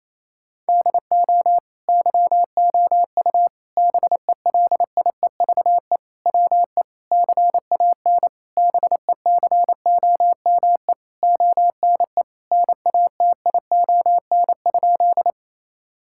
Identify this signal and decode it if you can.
Morse code, heard as a keyed tone, about 28 wpm, 700 Hz. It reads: DO YOU BELIEVE WE CAN BECOME ONE NATION?
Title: Sentences from Top 1000 Words - Review Part 6 - 28wpm